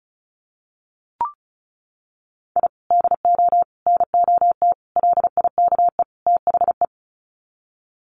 Morse code, heard as a keyed tone, about 34 wpm, 700 Hz. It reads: I DO NOT LIKE THE
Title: Sets of 5 Words Encore 35wpm